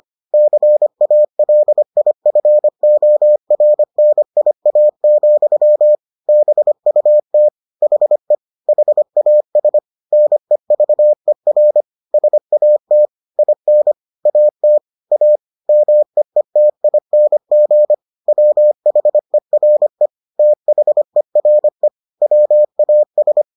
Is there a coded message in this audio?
Yes: Morse code from 0.3 to 23.4 s, heard as a keyed tone, about 25 wpm, 600 Hz. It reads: CALIFORNIA, BUT HE HAS NEVER SAT IN AT A MEETING WHERE THERE WAS